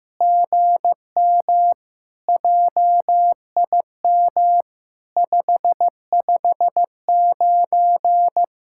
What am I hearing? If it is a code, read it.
Morse code, heard as a keyed tone, about 15 wpm, 700 Hz. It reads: GM JIM 559
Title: SOTA QSO Practice 15wpm